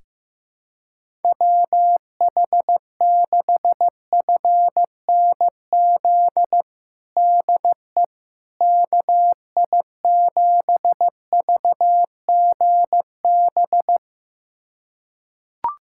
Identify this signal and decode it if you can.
Morse code, heard as a keyed tone, about 15 wpm, 700 Hz. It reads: WH6FNZ DE KI7VGB
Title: USA Callsign DE Callsign - 6 Characters - 15wpm